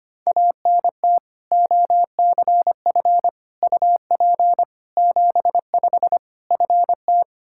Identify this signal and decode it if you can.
Morse code, heard as a keyed tone, about 25 wpm, 700 Hz. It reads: ANT OCF UP 75 FT
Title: Practice QSO - Exchange 2 - Info - 25wpm